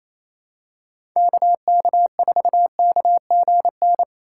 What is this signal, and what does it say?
Morse code, heard as a keyed tone, about 28 wpm, 700 Hz. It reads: KK4KGN
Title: USA Call Signs - 6 Characters - 4x Character Spacing - 28wpm